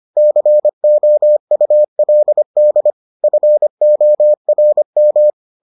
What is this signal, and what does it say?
Morse code, heard as a keyed tone, about 25 wpm, 600 Hz. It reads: COULD FORM